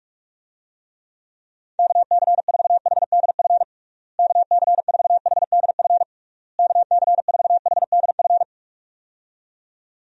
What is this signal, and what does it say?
Morse code, heard as a keyed tone, about 47 wpm, 700 Hz. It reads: KC4HDF KC4HDF KC4HDF
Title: USA Call Signs - 6 Characters - Repeated 3x - 45wpm